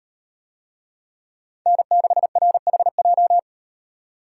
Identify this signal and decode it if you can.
Morse code, heard as a keyed tone, about 38 wpm, 700 Hz. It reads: N6RHJ